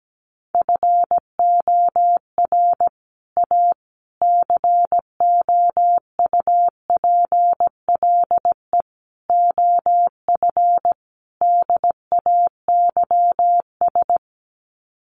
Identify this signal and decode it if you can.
Morse code, heard as a keyed tone, about 17 wpm, 700 Hz. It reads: FOR A COUPLE OF DAYS
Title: Sets of 5 Words Encore2 17wpm